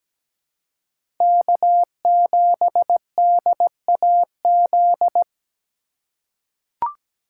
Morse code, heard as a keyed tone, about 17 wpm, 700 Hz. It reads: K7DAZ
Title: USA Call Signs - 5 Characters - 4x Character Spacing - 17wpm